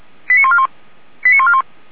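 Five-tone selective call (ZVEI), heard as quick runs of five tones, tones of about 70 ms. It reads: 89141 89141
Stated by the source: ZVEI five-tone selective call received via SDR.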